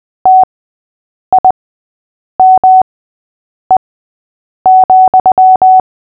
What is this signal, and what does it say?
Morse code, heard as a keyed tone, about 20 wpm, spaced as for 8 wpm, 750 Hz. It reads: TIME,
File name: cw / fists / Short-Story-6-8wpm